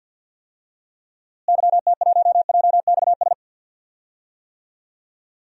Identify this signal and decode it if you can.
Morse code, heard as a keyed tone, about 48 wpm, 700 Hz. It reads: YT1JXS
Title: DXCC-100 Call Signs - 6 Characters - Repeated 2x - 50wpm